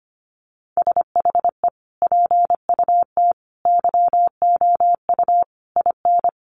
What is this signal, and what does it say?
Morse code, heard as a keyed tone, about 25 wpm, 700 Hz. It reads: SHE PUT YOU IN
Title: Sentences from Top 800 Words 25wpm